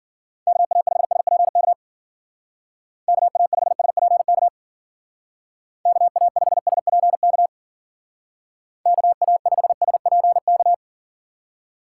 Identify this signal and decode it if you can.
Morse code, heard as a keyed tone, about 50 wpm, 700 Hz. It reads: KA5SPK KA5SPK KA5SPK KA5SPK
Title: Speed Racing - USA 2x3 Call Signs 40wpm